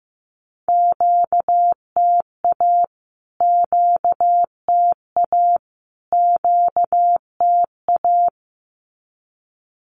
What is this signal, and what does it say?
Morse code, heard as a keyed tone, about 15 wpm, 700 Hz. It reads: QTA QTA QTA
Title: Q-Codes Repeated 3x 15wpm